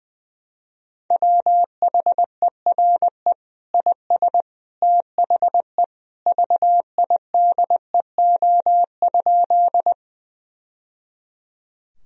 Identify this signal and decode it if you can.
Morse code, heard as a keyed tone, about 21 wpm, 700 Hz. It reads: WHERE IS THE VIDEO?
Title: Sentences from Top 1000 Words - Review Part 4 - 20wpm